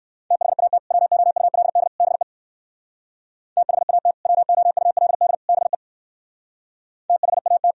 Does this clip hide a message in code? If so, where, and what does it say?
Yes: Morse code from 0.3 to 7.7 s, heard as a keyed tone, about 58 wpm, 700 Hz. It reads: THAT WOULD BE THAT WOULD BE THAT